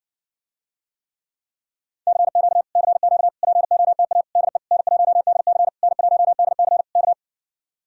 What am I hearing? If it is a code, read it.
Morse code, heard as a keyed tone, about 60 wpm, 700 Hz. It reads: CQ CQ POTA DE N1DC N1DC K